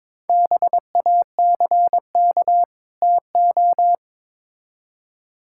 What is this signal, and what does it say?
Morse code, heard as a keyed tone, about 22 wpm, 700 Hz. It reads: BACK TO
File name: Build A Phrase - 2-Word Phrases - 22wpm